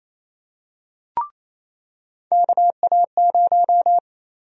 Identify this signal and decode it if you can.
Morse code, heard as a keyed tone, about 28 wpm, 700 Hz. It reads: KA0
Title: USA Call Sign Prefixes - Repeated 2x - 28wpm